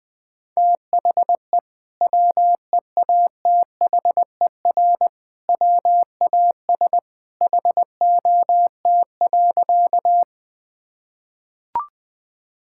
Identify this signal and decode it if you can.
Morse code, heard as a keyed tone, about 20 wpm, 700 Hz. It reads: THE WEATHER WAS HOT.
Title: Sentences from Top 1000 Words - Review Part 5 - 20wpm